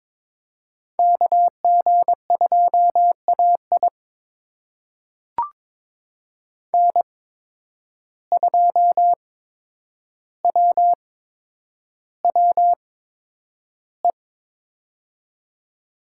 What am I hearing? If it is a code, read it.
Morse code, heard as a keyed tone, about 22 wpm, spaced as for 9 wpm, 700 Hz. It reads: KG2AI N2WWE
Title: USA Call Signs - 5 Characters - 8x Character Spacing - 22wpm